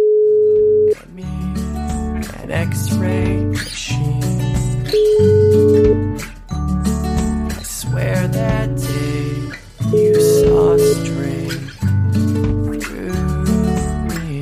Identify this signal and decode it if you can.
Ringback tone (Europe, 425 Hz), with speech.